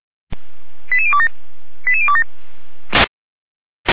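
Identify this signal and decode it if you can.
Five-tone selective call (ZVEI), heard as quick runs of five tones, tones of about 70 ms. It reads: 88027 88027
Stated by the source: ZVEI five-tone selective call received by SDR, feuerwehr-orlamuende.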